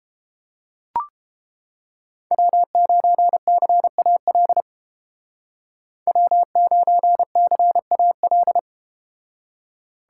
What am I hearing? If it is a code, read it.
Morse code, heard as a keyed tone, about 31 wpm, 700 Hz. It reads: W9CAL W9CAL